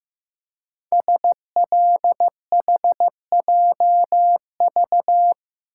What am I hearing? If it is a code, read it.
Morse code, heard as a keyed tone, about 15 wpm, 700 Hz. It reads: SLHJV
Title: Code Groups - 5 Letters - 15wpm